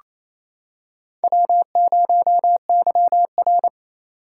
Morse code, heard as a keyed tone, about 28 wpm, 700 Hz. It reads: W0YR
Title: K1USN SST QSO Practice - 28wpm